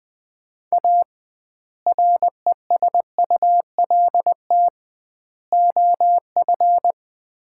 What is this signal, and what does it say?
Morse code, heard as a keyed tone, about 20 wpm, 700 Hz. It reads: A RESULT OF
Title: 2x Word-Spacing - Sets of 3 Words - 20wpm